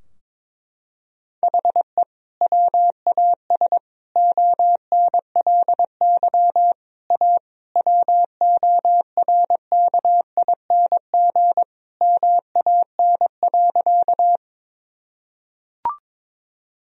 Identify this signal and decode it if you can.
Morse code, heard as a keyed tone, about 21 wpm, 700 Hz. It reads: HE WAS ONLY A WORKING MAN.
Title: Sentences from Top 1000 Words - Review Part 4 - 22wpm